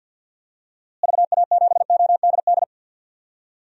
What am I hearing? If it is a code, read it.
Morse code, heard as a keyed tone, about 50 wpm, 700 Hz. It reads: VA7ODD
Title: DXCC-100 Call Signs - 6 Characters - Repeated 3x - 50wpm